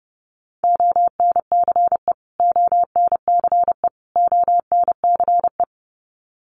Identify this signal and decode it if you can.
Morse code, heard as a keyed tone, about 30 wpm, 700 Hz. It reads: ONCE ONCE ONCE
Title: Top 300 Words Repeated 30wpm